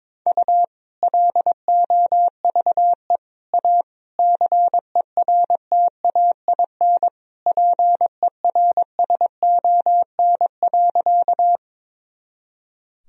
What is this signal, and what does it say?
Morse code, heard as a keyed tone, about 22 wpm, 700 Hz. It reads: U LOVE A CERTAIN PERSON.
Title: Sentences from Top 500 Words and Abbreviations 22wpm